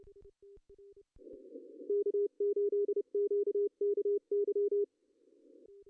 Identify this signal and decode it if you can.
Morse code, heard as a keyed tone, about 29 wpm, 410 Hz. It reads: K8QKY
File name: CW / cqww-cw-k5tr